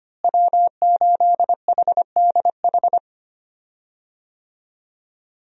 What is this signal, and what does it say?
Morse code, heard as a keyed tone, about 25 wpm, 700 Hz. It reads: W8HDH
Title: USA Call Signs - 5 Characters - Repeated 2x - 25wpm